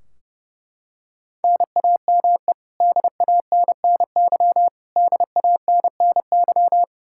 Morse code, heard as a keyed tone, about 30 wpm, 700 Hz. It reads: NAME DANNY DANNY
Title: Practice QSO - Exchange 1 - Beginning - 30wpm